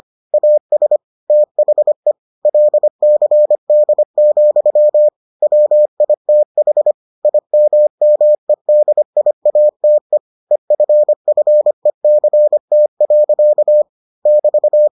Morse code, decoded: AS THE LCD, WITH IMMEDIATE EFFECT. <BT>

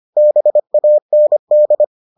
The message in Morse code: BAND

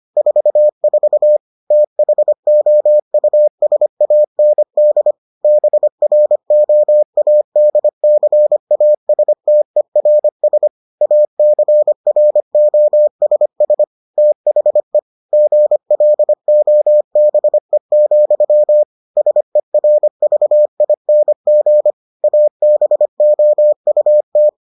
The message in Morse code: 44 THOUSAND BROADCASTERS ACROSS THE GLOBE, SERVING ABOUT

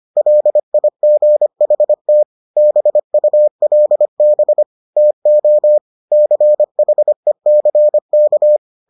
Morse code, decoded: LIGHT BULB TO CHECK

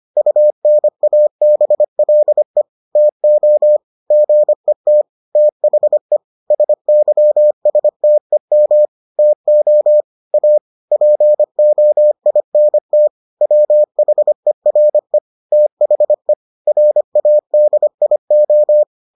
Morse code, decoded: UNABLE TO GET THE SYSTEM TO A POINT WHERE THE RADIO

